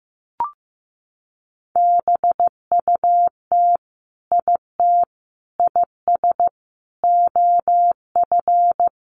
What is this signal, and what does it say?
Morse code, heard as a keyed tone, about 15 wpm, 700 Hz. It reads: BUT IT IS OF
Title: Sentences from Top 1200 Words 15wpm